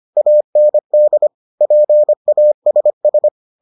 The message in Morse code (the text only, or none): AND PASS